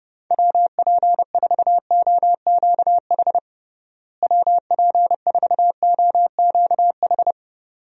Morse code, heard as a keyed tone, about 30 wpm, 700 Hz. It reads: WP4OQH WP4OQH
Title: USA Call Signs - 6 Characters - Repeated 2x - 30wpm